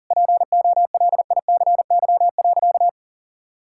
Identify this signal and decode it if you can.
Morse code, heard as a keyed tone, about 37 wpm, 700 Hz. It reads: POLICY.
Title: Sentences from Top 1000 Words - Review Part 5 - 40wpm